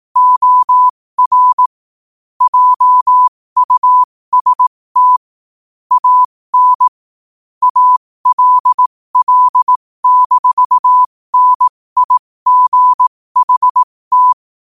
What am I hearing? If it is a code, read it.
Morse code, heard as a keyed tone, about 18 wpm, spaced as for 15 wpm, 1000 Hz. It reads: OR JUST AN ALL-NIGHT
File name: qotdfiles / 15wpm